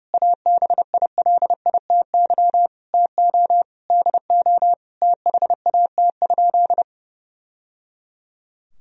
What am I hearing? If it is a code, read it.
Morse code, heard as a keyed tone, about 32 wpm, 700 Hz. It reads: ABILITY TO DO THAT?
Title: Sentences from Top 1000 Words - Review Part 3 - 30wpm